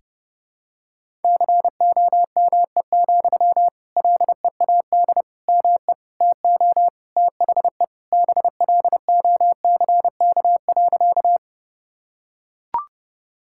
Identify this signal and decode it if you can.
Morse code, heard as a keyed tone, about 30 wpm, 700 Hz. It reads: COME, LEAD ME TO THE BLOCK.
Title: Sentences from Top 1600 Words 30wpm